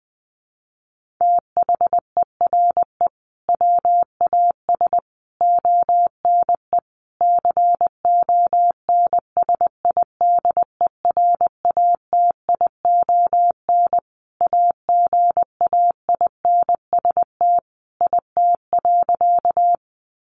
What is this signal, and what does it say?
Morse code, heard as a keyed tone, about 20 wpm, 700 Hz. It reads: THERE WAS ONE CONSIDERATION AGAINST IT.
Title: Sentences from Top 1500 Words 20wpm